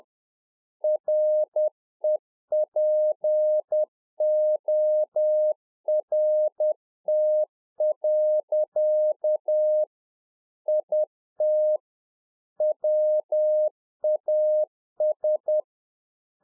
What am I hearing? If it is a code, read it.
Morse code, heard as a keyed tone, about 10 wpm, 610 Hz. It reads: REPORT. IT WAS
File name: UNFOGETTABLE QSO 10WPM